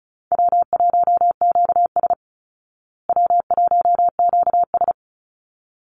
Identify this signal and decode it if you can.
Morse code, heard as a keyed tone, about 34 wpm, 700 Hz. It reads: W1QS W1QS
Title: Easy Call Signs Repeated 2x - 35wpm